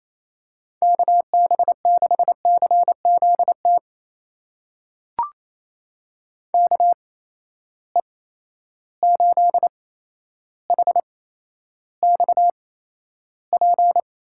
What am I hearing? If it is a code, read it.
Morse code, heard as a keyed tone, about 27 wpm, spaced as for 12 wpm, 700 Hz. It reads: KB6CZT KE8HXP
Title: USA Call Signs - 6 Characters - 8x Character Spacing - 28wpm